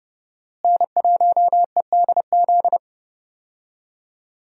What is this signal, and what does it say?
Morse code, heard as a keyed tone, about 30 wpm, 700 Hz. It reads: N1EDZ
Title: USA Call Signs - 5 Characters - Repeated 3x - 30wpm